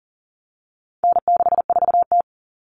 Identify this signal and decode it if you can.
Morse code, heard as a keyed tone, about 40 wpm, 700 Hz. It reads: N64T
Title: Common Cut Numbers - 4-Digit 40wpm